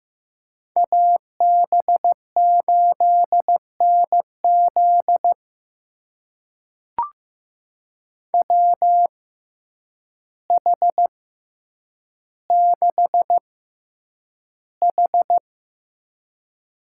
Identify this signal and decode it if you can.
Morse code, heard as a keyed tone, about 15 wpm, spaced as for 8 wpm, 700 Hz. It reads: AB8NZ WH6H
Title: USA Call Signs - 5 Characters - 6x Character Spacing - 15wpm